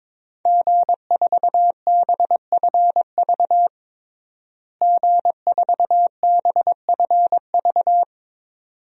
Morse code, heard as a keyed tone, about 21 wpm, 700 Hz. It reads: G4BFV G4BFV
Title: DXCC-100 Call Signs - 5 Characters - Repeated 2x - 22wpm